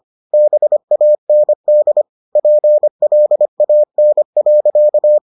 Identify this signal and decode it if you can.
Morse code, heard as a keyed tone, about 25 wpm, 600 Hz. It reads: BAND PLAN.